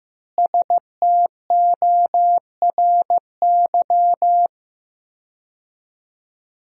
Morse code, heard as a keyed tone, about 15 wpm, 700 Hz. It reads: STORY
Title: 4x Word-Spacing - Sentences from Top 300 Words - 15wpm